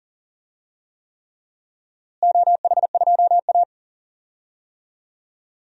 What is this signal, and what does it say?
Morse code, heard as a keyed tone, about 40 wpm, 700 Hz. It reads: OH2A
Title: DXCC-100 Call Signs - 4 Characters - Repeated 3x - 40wpm